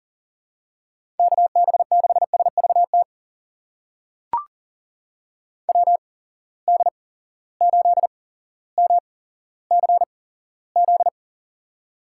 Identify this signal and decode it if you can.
Morse code, heard as a keyed tone, about 40 wpm, spaced as for 17 wpm, 700 Hz. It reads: KB6SVT WD8MCZ
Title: USA Call Signs - 6 Characters - 8x Character Spacing - 40wpm